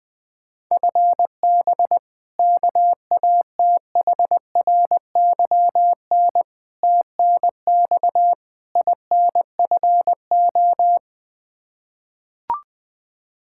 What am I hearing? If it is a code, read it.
Morse code, heard as a keyed tone, about 20 wpm, 700 Hz. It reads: FB KATHRYN TNX INFO
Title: Practice QSO - Exchange 3 - More Info - 20wpm